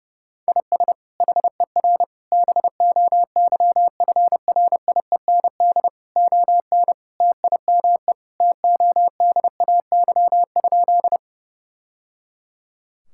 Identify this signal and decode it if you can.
Morse code, heard as a keyed tone, about 32 wpm, 700 Hz. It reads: IS HER BOYFRIEND ON TIME TODAY?
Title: Sentences from Top 1000 Words - Review Part 5 - 30wpm